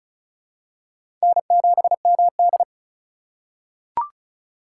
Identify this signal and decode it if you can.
Morse code, heard as a keyed tone, about 34 wpm, 700 Hz. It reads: N7MD
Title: USA Call Signs - 4 Characters - 6x Character Spacing - 35wpm